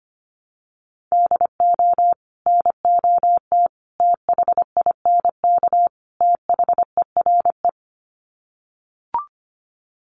Morse code, heard as a keyed tone, about 25 wpm, 700 Hz. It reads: DO NOT THINK THERE
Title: Sets of 4 Words 25wpm